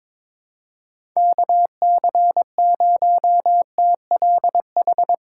Morse code, heard as a keyed tone, about 22 wpm, 700 Hz. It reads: KC0TLH
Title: USA Call Signs - 6 Characters - 3x Character Spacing - 22wpm